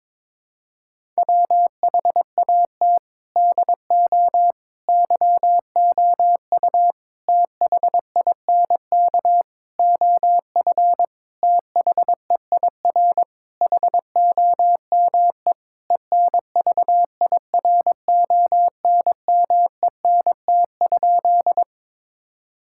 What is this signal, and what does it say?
Morse code, heard as a keyed tone, about 22 wpm, 700 Hz. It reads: WHAT DO YOU THINK OF THEIR HOME ENVIRONMENT?
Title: Sentences from Top 1000 Words - Review Part 5 - 22wpm